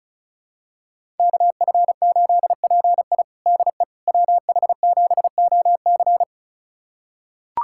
Morse code, heard as a keyed tone, about 36 wpm, 700 Hz. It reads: KF8PI DE WH7OC
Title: USA Callsign DE Callsign - 5 Characters - 35wpm